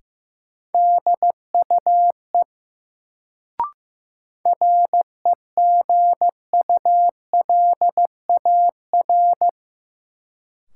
Morse code, heard as a keyed tone, about 15 wpm, 700 Hz. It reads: DUE REGULAR
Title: Sentences from Top 900 Words - New Word Drill - 15wpm